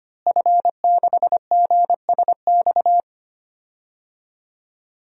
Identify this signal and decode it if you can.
Morse code, heard as a keyed tone, about 25 wpm, 700 Hz. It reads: F6GSX
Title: DXCC-100 Call Signs - 5 Characters - Repeated 2x - 25wpm